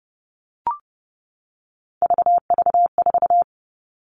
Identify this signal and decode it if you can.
Morse code, heard as a keyed tone, about 30 wpm, 700 Hz. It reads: VV4
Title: All Cut Numbers - 3-Digit 30wpm